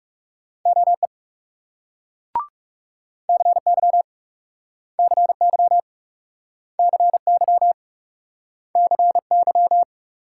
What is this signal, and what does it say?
Morse code, heard as a keyed tone, about 38 wpm, 700 Hz. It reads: OE CY CY CY CY